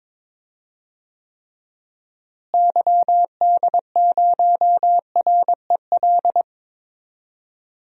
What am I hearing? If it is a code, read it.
Morse code, heard as a keyed tone, about 22 wpm, 700 Hz. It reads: YD0REL
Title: DXCC-100 Call Signs - 6 Characters - Repeated 2x - 22wpm